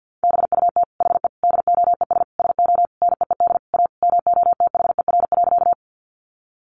Morse code, heard as a keyed tone, about 50 wpm, 700 Hz. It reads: BUT HE DOES SO NEED A MOTHER.
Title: Sentences from Top 200 Words 50wpm